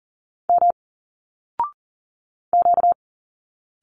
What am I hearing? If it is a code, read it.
Morse code, heard as a keyed tone, about 40 wpm, 700 Hz. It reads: M Q
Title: Single Letters 40wpm